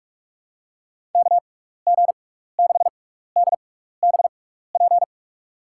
Morse code, heard as a keyed tone, about 45 wpm, spaced as for 15 wpm, 700 Hz. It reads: KG6DBP
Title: USA Call Signs - 6x Character Spacing - 45wpm